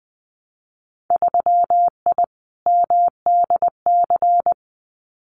Morse code, heard as a keyed tone, about 20 wpm, 700 Hz. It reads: 3I MDC
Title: Winter Field Day Exchanges Encore 20wpm